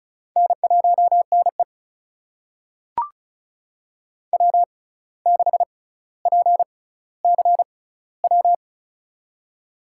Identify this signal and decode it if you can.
Morse code, heard as a keyed tone, about 36 wpm, spaced as for 17 wpm, 700 Hz. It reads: N1NE W6PCW
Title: USA Call Signs - 6x Character Spacing - 35wpm